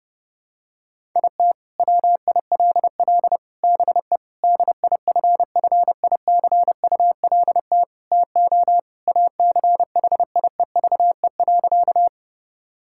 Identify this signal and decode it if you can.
Morse code, heard as a keyed tone, about 30 wpm, 700 Hz. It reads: IT WILL BE DIFFICULT TO ACHIEVE.
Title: Sentences from Top 1000 Words - Review Part 6 - 30wpm